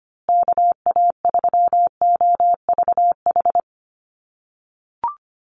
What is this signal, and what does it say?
Morse code, heard as a keyed tone, about 25 wpm, 700 Hz. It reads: KA3OVH
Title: USA Call Signs Repeated 2x 25wpm